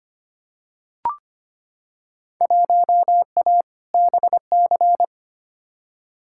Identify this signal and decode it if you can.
Morse code, heard as a keyed tone, about 25 wpm, 700 Hz. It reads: 1A BC
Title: Field Day Exchanges - 25wpm